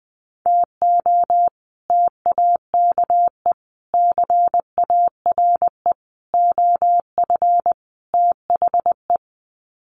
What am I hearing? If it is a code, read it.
Morse code, heard as a keyed tone, about 20 wpm, 700 Hz. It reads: TO TAKE CARE OF THE
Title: Sets of 5 Words Encore 20wpm